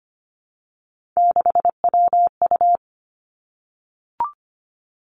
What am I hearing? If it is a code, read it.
Morse code, heard as a keyed tone, about 25 wpm, 700 Hz. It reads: 6WU
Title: Three Challenging Letter-Number Combinations 25wpm